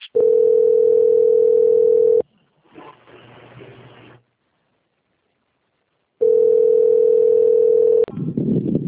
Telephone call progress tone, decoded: ringback tone

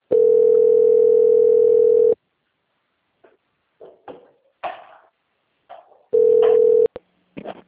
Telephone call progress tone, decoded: ringback tone

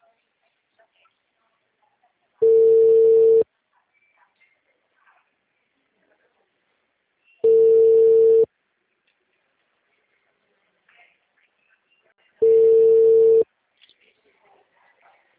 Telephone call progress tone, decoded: ringback tone

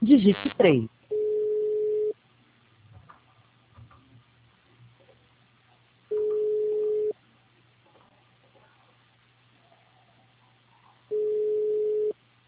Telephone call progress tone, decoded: ringback tone

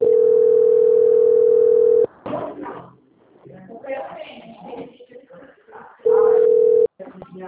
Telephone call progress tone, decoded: ringback tone

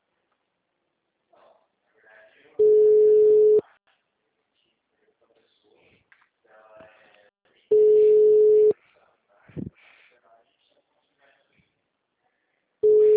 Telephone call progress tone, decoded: ringback tone

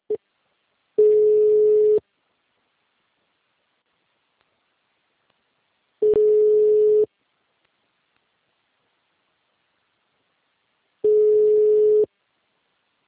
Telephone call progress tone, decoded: ringback tone

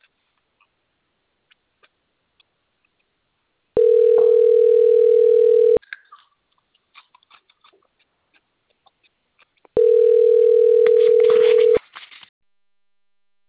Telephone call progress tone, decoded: ringback tone

